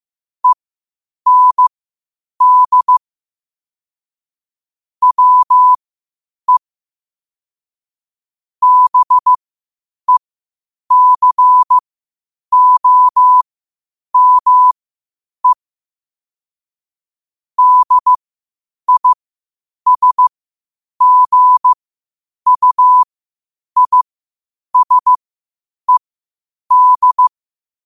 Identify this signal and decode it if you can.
Morse code, heard as a keyed tone, about 15 wpm, spaced as for 8 wpm, 1000 Hz. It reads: END WE BECOME DISGUISED
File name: qotdfiles / 8wpm